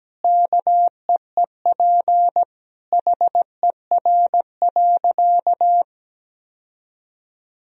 Morse code, heard as a keyed tone, about 17 wpm, 700 Hz. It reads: KEEP HER.
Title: Sentences from Top 1000 Words - Review Part 2 - 17wpm